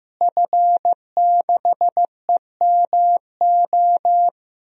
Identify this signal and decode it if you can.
Morse code, heard as a keyed tone, about 15 wpm, 700 Hz. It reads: F6EMO
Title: DXCC-100 Call Signs - 5 Characters - Repeated 3x - 15wpm